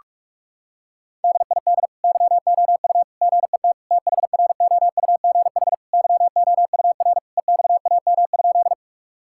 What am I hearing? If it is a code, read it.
Morse code, heard as a keyed tone, about 46 wpm, 690 Hz. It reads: DID YOU GET THROUGH YOUR EXAM?
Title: Sentences from Top 1000 Words - Review Part 5 - 45wpm